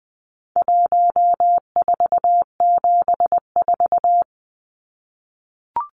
Morse code, heard as a keyed tone, about 20 wpm, 700 Hz. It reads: 1474
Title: Numbers - 4-Digit 20wpm